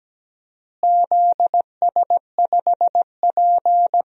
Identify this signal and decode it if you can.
Morse code, heard as a keyed tone, about 17 wpm, 700 Hz. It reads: ZS5P
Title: DXCC-100 Call Signs - 4 Characters - Repeated 2x - 17wpm